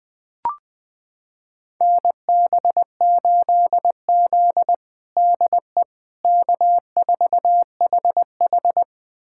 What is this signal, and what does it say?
Morse code, heard as a keyed tone, about 20 wpm, 700 Hz. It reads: NB8Z DE K4HH
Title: USA Callsign DE Callsign - 4 Characters - 20wpm